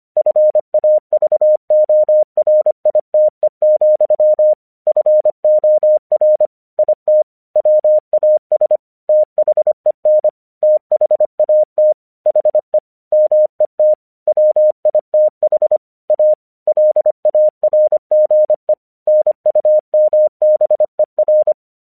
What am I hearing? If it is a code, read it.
Morse code, heard as a keyed tone, about 24 wpm, 610 Hz. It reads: FAVORITE, FOR IT WAS THEN THAT HE MET WITH A LARGE NUMBER
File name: 1 English 25wpm / AQC13